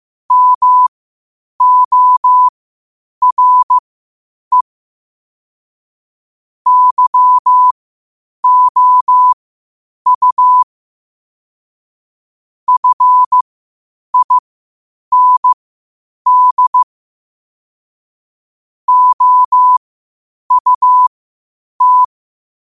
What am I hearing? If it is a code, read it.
Morse code, heard as a keyed tone, about 15 wpm, spaced as for 8 wpm, 1000 Hz. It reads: MORE YOU FIND OUT
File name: qotdfiles / 8wpm